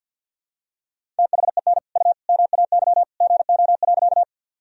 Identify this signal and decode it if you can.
Morse code, heard as a keyed tone, about 51 wpm, 700 Hz. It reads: THEN U MAY GO.
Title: Sentences from Top 100 Words and Abbreviations 50wpm